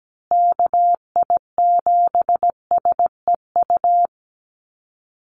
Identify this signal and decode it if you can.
Morse code, heard as a keyed tone, about 17 wpm, 700 Hz. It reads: KI7SEU
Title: USA Call Signs Repeated 3x 17wpm